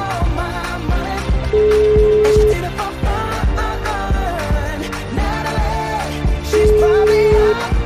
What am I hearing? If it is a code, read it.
Ringback tone (Europe, 425 Hz), with noise.